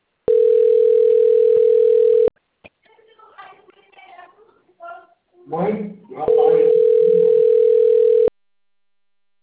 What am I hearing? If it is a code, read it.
Ringback tone (North America), 440 + 480 Hz, with speech.
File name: rg-1600-unknown- - - .75083